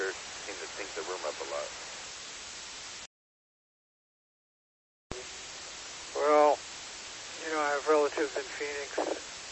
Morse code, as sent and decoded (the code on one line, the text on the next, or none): none